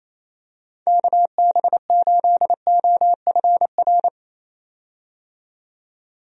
Morse code, heard as a keyed tone, about 28 wpm, 700 Hz. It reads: KB8OFR
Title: USA Call Signs - 6 Characters - 28wpm